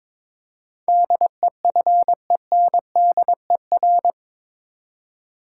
Morse code, heard as a keyed tone, about 21 wpm, 700 Hz. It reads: DEFENDER
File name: practice / 22wpm / Eight Letter Words - 22wpm